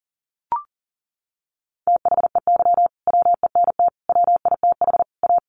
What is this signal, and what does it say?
Morse code, heard as a keyed tone, about 40 wpm, 700 Hz. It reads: THEY WENT WITH A
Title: Sentences from Top 800 Words 40wpm